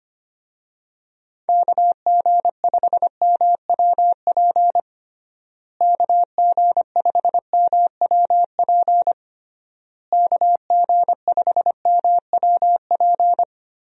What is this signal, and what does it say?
Morse code, heard as a keyed tone, about 25 wpm, 700 Hz. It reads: KG5MWP KG5MWP KG5MWP
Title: USA Call Signs - 6 Characters - Repeated 3x - 25wpm